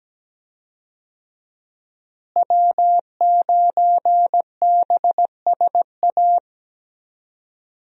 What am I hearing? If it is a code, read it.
Morse code, heard as a keyed tone, about 17 wpm, 700 Hz. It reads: W9BSA
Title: USA Call Signs - 5 Characters - Repeated 2x - 17wpm